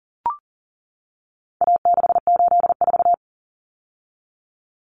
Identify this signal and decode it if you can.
Morse code, heard as a keyed tone, about 40 wpm, 700 Hz. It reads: A684
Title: Common Cut Numbers - 4-Digit 40wpm